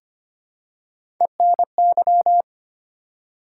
Morse code, heard as a keyed tone, about 25 wpm, 700 Hz. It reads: ENY
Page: morse-code-ninja-practice-25wpm